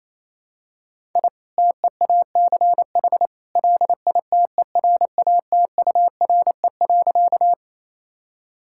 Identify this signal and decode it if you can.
Morse code, heard as a keyed tone, about 27 wpm, 700 Hz. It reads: I TEACH LITERATURE.
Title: Sentences from Top 1000 Words - Review Part 6 - 28wpm